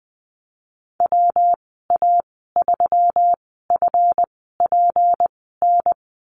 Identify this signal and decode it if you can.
Morse code, heard as a keyed tone, about 19 wpm, spaced as for 14 wpm, 700 Hz. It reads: WA3FPN
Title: 167 Lesson 27g - Number 7 - Call Signs 20wpm